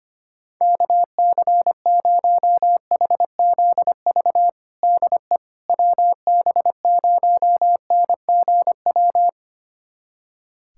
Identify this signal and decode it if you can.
Morse code, heard as a keyed tone, about 25 wpm, 700 Hz. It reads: KC0HZV DE WB0NGW
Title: USA Callsign DE Callsign - 6 Characters - 25wpm